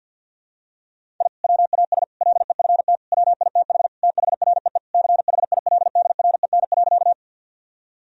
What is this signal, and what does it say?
Morse code, heard as a keyed tone, about 51 wpm, 700 Hz. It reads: I WAS LEFT WITH THREE CHILDREN.
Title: Sentences from Top 1000 Words - Review Part 1 - 50wpm